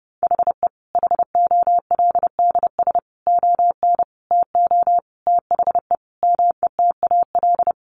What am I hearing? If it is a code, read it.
Morse code, heard as a keyed tone, about 30 wpm, 700 Hz. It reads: HE HOLDS ON TO THE METAL
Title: Additional Easy Phrases 30wpm